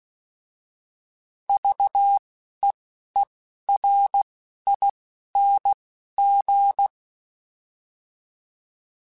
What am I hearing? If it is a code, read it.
Morse code, heard as a keyed tone, about 16 wpm, spaced as for 11 wpm, 800 Hz. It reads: VEERING
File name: Alphabetical / Alphabetical mp3s / Vs @10